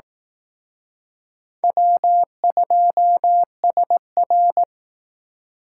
Morse code, heard as a keyed tone, about 19 wpm, 700 Hz. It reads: W2SR